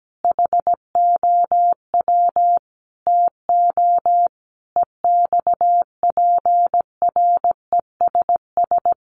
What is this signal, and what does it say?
Morse code, heard as a keyed tone, about 17 wpm, 700 Hz. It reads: HOW TO EXPRESS
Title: Sentences from Top 1100 Words 17wpm